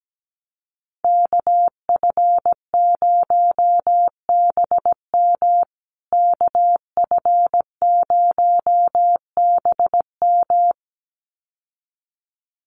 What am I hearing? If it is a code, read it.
Morse code, heard as a keyed tone, about 17 wpm, 700 Hz. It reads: KF0BM KF0BM
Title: USA Call Signs Repeated 2x 17wpm